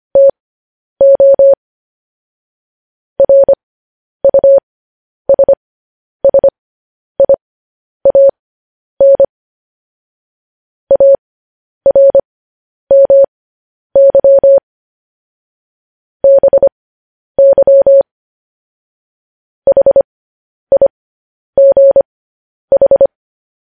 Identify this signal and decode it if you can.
Morse code, heard as a keyed tone, about 25 wpm, spaced as for 10 wpm, 550 Hz. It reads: TO RUSSIAN ARMY BY HIGH